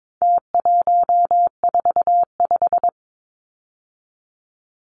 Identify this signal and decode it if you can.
Morse code, heard as a keyed tone, about 22 wpm, 700 Hz. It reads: T145
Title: Numbers - Contest Style - 4-Digit 22wpm